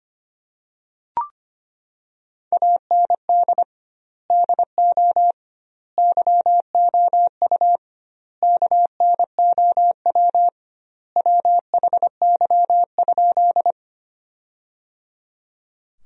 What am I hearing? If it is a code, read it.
Morse code, heard as a keyed tone, about 25 wpm, 700 Hz. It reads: AND DO YOU KNOW WHY?
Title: 2x Word-Spacing - Sentences from Top 200 Words - 25wpm